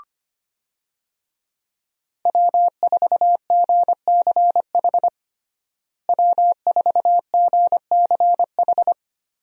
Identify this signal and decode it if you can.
Morse code, heard as a keyed tone, about 25 wpm, 700 Hz. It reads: W4GCH W4GCH
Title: USA Call Signs - 5 Characters - Repeated 3x - 25wpm